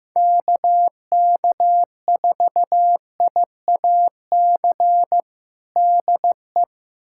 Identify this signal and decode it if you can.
Morse code, heard as a keyed tone, about 16 wpm, 700 Hz. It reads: KK4IAC DE
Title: USA Callsign DE Callsign 15wpm